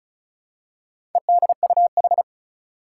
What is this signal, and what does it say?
Morse code, heard as a keyed tone, about 36 wpm, 700 Hz. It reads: EDUH
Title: Four Challenging Letter-Number Combinations 35wpm